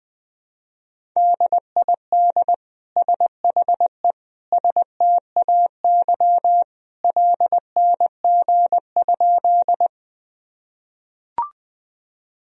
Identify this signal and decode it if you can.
Morse code, heard as a keyed tone, about 20 wpm, 700 Hz. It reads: DID SHE STAY LNG?
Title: Sentences from Top 500 Words and Abbreviations 20wpm